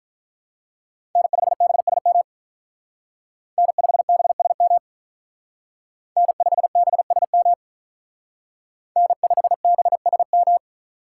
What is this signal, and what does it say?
Morse code, heard as a keyed tone, about 46 wpm, 700 Hz. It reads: N5BSM N5BSM N5BSM N5BSM